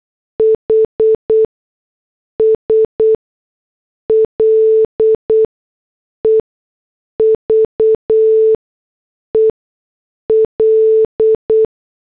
Morse code, heard as a keyed tone, about 8 wpm, spaced as for 6 wpm, 440 Hz. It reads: HS LEVEL
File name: morse / headlines